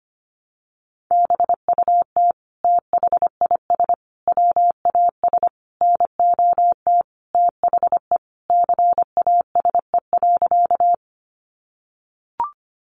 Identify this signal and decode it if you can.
Morse code, heard as a keyed tone, about 25 wpm, 700 Hz. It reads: BUT THIS WAS NOT THE CASE.
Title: Sentences from Top 700 Words 25wpm